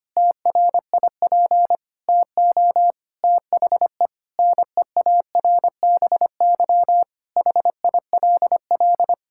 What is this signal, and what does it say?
Morse code, heard as a keyed tone, about 25 wpm, 700 Hz. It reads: TRIP TO THE NEARBY HILL
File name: practice / 25wpm / Difficult Sentences with Hints - 25wpm